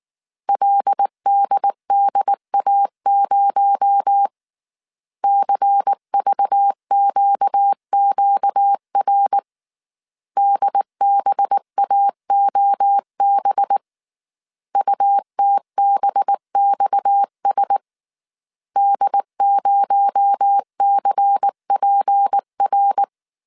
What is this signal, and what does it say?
Morse code, heard as a keyed tone, about 19 wpm, 800 Hz. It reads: LDDA0 CVQQR DBAOB UTBXS D0CPR
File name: CW / DET / 1 6 1500C S A 18wpm 2255